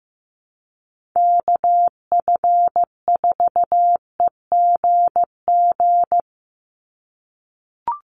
Morse code, heard as a keyed tone, about 15 wpm, 700 Hz. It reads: KF4EGG
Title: USA Call Signs 15wpm